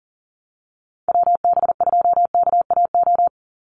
Morse code, heard as a keyed tone, about 40 wpm, 700 Hz. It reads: WB2KAO
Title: Call Signs Repeated 3x - 40wpm